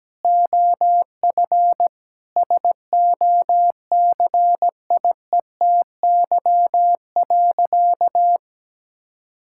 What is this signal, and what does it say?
Morse code, heard as a keyed tone, about 17 wpm, 700 Hz. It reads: OF SOCIETY.
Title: Sentences from Top 1000 Words - Review Part 5 - 17wpm